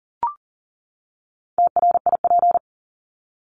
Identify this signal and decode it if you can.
Morse code, heard as a keyed tone, about 40 wpm, 700 Hz. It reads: TRIP
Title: Top 1251-1500 Words 40wpm